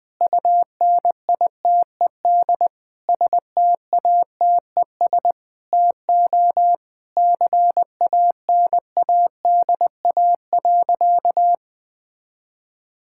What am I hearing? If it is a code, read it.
Morse code, heard as a keyed tone, about 21 wpm, 700 Hz. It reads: UNITED STATES TO CANADA.
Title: Sentences with 4-Word Phrases - 20wpm